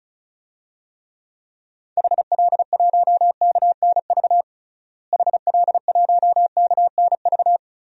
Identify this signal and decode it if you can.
Morse code, heard as a keyed tone, about 36 wpm, 700 Hz. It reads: HL1KNV HL1KNV
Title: DXCC-100 Call Signs - 6 Characters - Repeated 3x - 35wpm